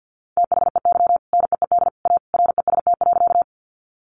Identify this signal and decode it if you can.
Morse code, heard as a keyed tone, about 50 wpm, 700 Hz. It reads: THEY NEED A REST.
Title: Sentences from Top 500 Words 50wpm